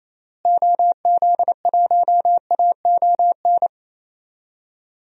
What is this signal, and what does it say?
Morse code, heard as a keyed tone, about 28 wpm, 700 Hz. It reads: OZ1AON
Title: DXCC-100 Call Signs - 6 Characters - Repeated 3x - 28wpm